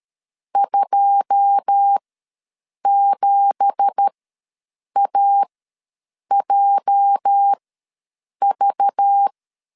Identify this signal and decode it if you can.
Morse code, heard as a keyed tone, about 13 wpm, spaced as for 7 wpm, 800 Hz. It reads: 27AJV
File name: LEZIONI / 9 2867 ws 12wpm 3431